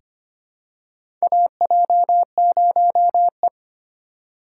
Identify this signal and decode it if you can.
Morse code, heard as a keyed tone, about 25 wpm, 700 Hz. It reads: AJ0E
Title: USA Call Signs - 3x Character Spacing - 25wpm